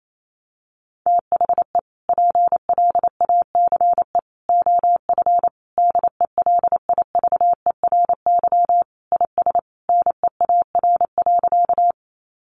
Morse code, heard as a keyed tone, about 28 wpm, 700 Hz. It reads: THE PLACE OF DELIVERY IS NEAR.
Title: Sentences from Top 1500 Words 28wpm